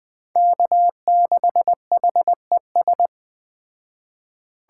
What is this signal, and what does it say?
Morse code, heard as a keyed tone, about 21 wpm, 700 Hz. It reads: K6HES